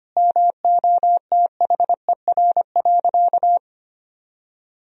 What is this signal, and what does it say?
Morse code, heard as a keyed tone, about 25 wpm, 700 Hz. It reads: MOTHER.
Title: Sentences from Top 1000 Words - Review Part 2 - 25wpm